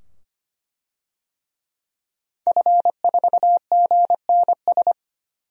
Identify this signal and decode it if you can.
Morse code, heard as a keyed tone, about 25 wpm, 700 Hz. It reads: F4GNS
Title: DXCC-100 Call Signs - 5 Characters - 25wpm